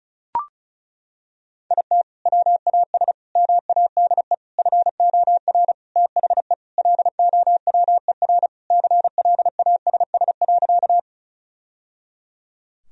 Morse code, heard as a keyed tone, about 34 wpm, 700 Hz. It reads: IT WAS MADE FOR THE LOWER CLASS.
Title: Sentences from Top 1000 Words - Review Part 4 - 35wpm